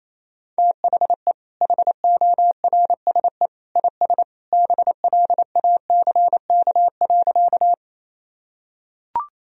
Morse code, heard as a keyed tone, about 28 wpm, 700 Hz. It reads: THE HORSE IS BLACK.
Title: Sentences from Top 1000 Words - Review Part 2 - 28wpm